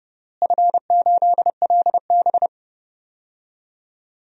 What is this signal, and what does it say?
Morse code, heard as a keyed tone, about 30 wpm, 700 Hz. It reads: F8LB
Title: DXCC-100 Call Signs - 4 Characters - Repeated 3x - 30wpm